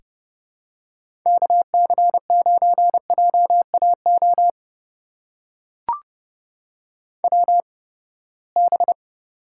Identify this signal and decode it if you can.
Morse code, heard as a keyed tone, about 32 wpm, spaced as for 21 wpm, 700 Hz. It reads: KC9JAO WB
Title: USA Call Signs - 6 Characters - 8x Character Spacing - 30wpm